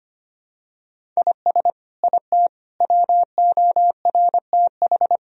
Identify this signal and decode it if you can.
Morse code, heard as a keyed tone, about 25 wpm, 700 Hz. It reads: IS IT WORTH
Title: Sentences from Top 1000 Words - Review Part 1 - 25wpm